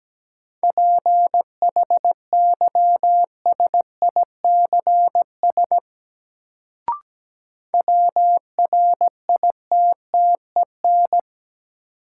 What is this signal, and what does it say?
Morse code, heard as a keyed tone, about 17 wpm, 700 Hz. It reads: PHYSICS WRITTEN
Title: Sentences from Top 1500 Words - New Word Drill Alt - 17wpm